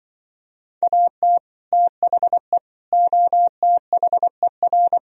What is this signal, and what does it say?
Morse code, heard as a keyed tone, about 24 wpm, 700 Hz. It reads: AT THE OTHER
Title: Speed-Racing - Sets of 5 Words 20wpm